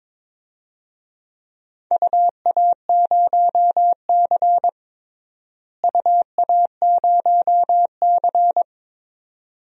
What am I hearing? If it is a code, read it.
Morse code, heard as a keyed tone, about 22 wpm, 700 Hz. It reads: UA0C UA0C
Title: DXCC-100 Call Signs - 4 Characters - Repeated 2x - 22wpm